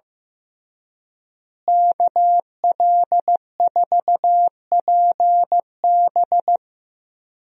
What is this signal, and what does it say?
Morse code, heard as a keyed tone, about 15 wpm, 700 Hz. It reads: KL4PB
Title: USA Call Signs - 5 Characters - Repeated 3x - 15wpm